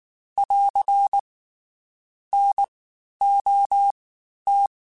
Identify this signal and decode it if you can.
Morse code, heard as a keyed tone, about 19 wpm, spaced as for 11 wpm, 800 Hz. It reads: <AR> NOT
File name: Special INGs-mp3s / Special INGs-mp3s / Fast ing @18